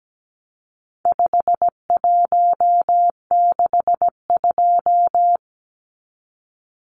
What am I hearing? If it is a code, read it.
Morse code, heard as a keyed tone, about 17 wpm, 700 Hz. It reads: 5162
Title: Numbers - 4-Digit 17wpm